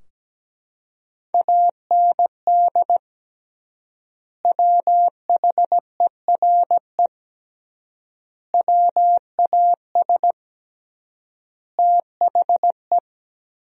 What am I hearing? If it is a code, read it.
Morse code, heard as a keyed tone, about 17 wpm, 700 Hz. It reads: AND WHERE WAS THE
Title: 3x Word-Spacing - Sentences from Top 200 Words - 17wpm